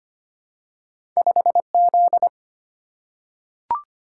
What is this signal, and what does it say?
Morse code, heard as a keyed tone, about 25 wpm, 700 Hz. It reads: 5Z
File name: Two Letter-Number Combinations - 25wpm